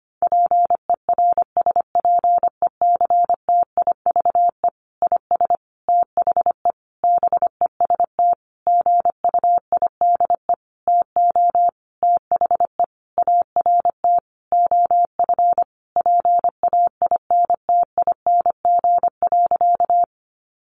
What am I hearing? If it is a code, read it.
Morse code, heard as a keyed tone, about 25 wpm, 700 Hz. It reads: PERSPECTIVE IS THE BEST GUIDE TO THE ART OF PAINTING.
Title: Sentences from Top 1200 Words 25wpm